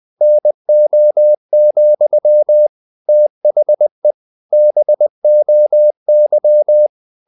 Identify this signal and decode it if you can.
Morse code, heard as a keyed tone, about 20 wpm, 600 Hz. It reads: NO, THE BOY
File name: the old man and the sea / the old man and the sea part 1 20wpm